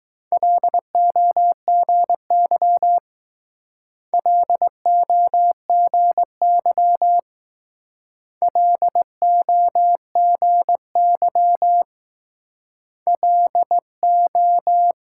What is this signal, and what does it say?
Morse code, heard as a keyed tone, about 21 wpm, 700 Hz. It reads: LOGY LOGY LOGY LO